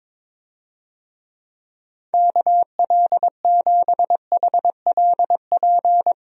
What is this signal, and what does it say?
Morse code, heard as a keyed tone, about 22 wpm, 700 Hz. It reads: KL7HLP
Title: DXCC-100 Call Signs - 6 Characters - Repeated 3x - 22wpm